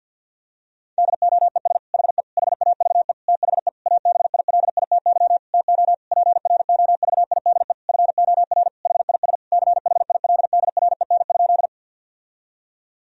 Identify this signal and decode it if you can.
Morse code, heard as a keyed tone, about 50 wpm, 700 Hz. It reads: DOES HE HAVE THE ABILITY TO PROVIDE FOR HIS CHILDREN?
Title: Sentences from Top 1000 Words - Review Part 3 - 50wpm